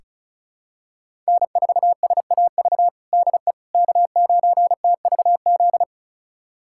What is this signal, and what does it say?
Morse code, heard as a keyed tone, about 34 wpm, 700 Hz. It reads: N4SAV DE K9TVZ
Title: USA Callsign DE Callsign - 5 Characters - 35wpm